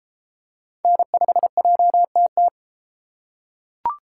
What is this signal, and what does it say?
Morse code, heard as a keyed tone, about 32 wpm, 700 Hz. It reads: N5JTT